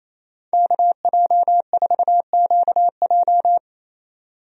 Morse code, heard as a keyed tone, about 27 wpm, 700 Hz. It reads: KJ4QJ
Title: USA Call Signs - 5 Characters - 28wpm